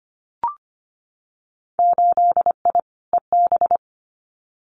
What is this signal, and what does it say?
Morse code, heard as a keyed tone, about 25 wpm, 690 Hz. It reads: 8I EB
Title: Winter Field Day Exchanges Encore 25wpm